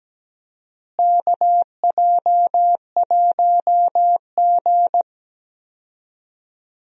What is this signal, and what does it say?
Morse code, heard as a keyed tone, about 17 wpm, 700 Hz. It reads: KJ1G
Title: USA Call Signs - 4 Characters - Repeated 3x - 17wpm